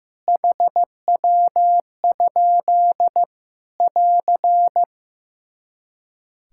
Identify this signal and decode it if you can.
Morse code, heard as a keyed tone, about 16 wpm, 700 Hz. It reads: HW? <AR>
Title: Practice QSO - Exchange 3 - More Info - 15wpm